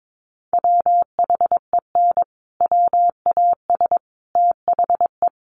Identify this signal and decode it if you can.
Morse code, heard as a keyed tone, about 22 wpm, 700 Hz. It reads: WHEN WAS THE
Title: Sets of 5 Words Encore 22wpm